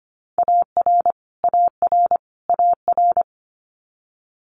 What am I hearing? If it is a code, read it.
Morse code, heard as a keyed tone, about 25 wpm, 700 Hz. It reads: AR AR AR
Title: Multi-letter Phonemes Repeated 3x 25wpm